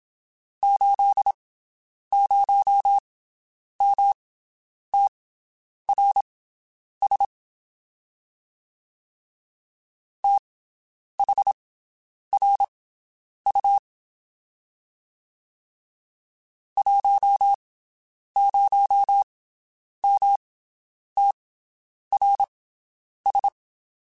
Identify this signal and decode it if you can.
Morse code, heard as a keyed tone, about 27 wpm, spaced as for 9 wpm, 800 Hz. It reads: 80MTRS THRU 10MTRS
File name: QSOs Spaced